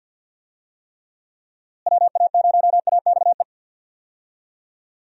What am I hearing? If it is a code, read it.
Morse code, heard as a keyed tone, about 51 wpm, 700 Hz. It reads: WA0AXE
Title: USA Call Signs - 6 Characters - Repeated 2x - 50wpm